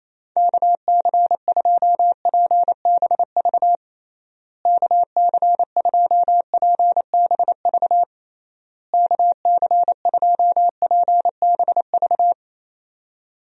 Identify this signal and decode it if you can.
Morse code, heard as a keyed tone, about 28 wpm, 700 Hz. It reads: KC2PBV KC2PBV KC2PBV
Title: USA Call Signs - 6 Characters - Repeated 3x - 28wpm